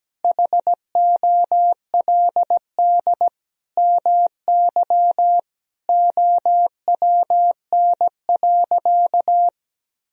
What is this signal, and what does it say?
Morse code, heard as a keyed tone, about 17 wpm, 700 Hz. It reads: HOLD MY OWN.
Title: Sentences from Top 1000 Words - Review Part 2 - 17wpm